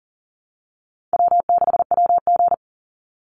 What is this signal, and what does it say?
Morse code, heard as a keyed tone, about 40 wpm, 700 Hz. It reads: W6WG
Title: Easy Call Signs - 40wpm